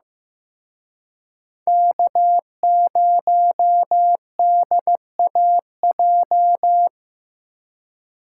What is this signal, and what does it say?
Morse code, heard as a keyed tone, about 18 wpm, 700 Hz. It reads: K0DAJ
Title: Speed Racing - USA 1x3 Call Signs 15wpm